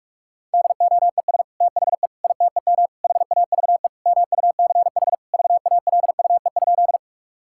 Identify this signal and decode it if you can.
Morse code, heard as a keyed tone, about 42 wpm, 700 Hz. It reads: DOES THE ITEM HAVE MUCH VALUE?
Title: Sentences from Top 1000 Words - Review Part 6 - 45wpm